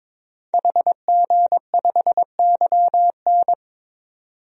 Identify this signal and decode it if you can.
Morse code, heard as a keyed tone, about 22 wpm, 700 Hz. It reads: HG5YN
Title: DXCC-100 Call Signs - 5 Characters - 22wpm